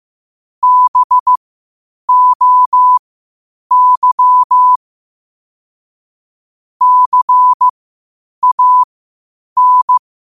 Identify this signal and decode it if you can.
Morse code, heard as a keyed tone, about 15 wpm, spaced as for 8 wpm, 1000 Hz. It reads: BOY CAN
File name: qotdfiles / 8wpm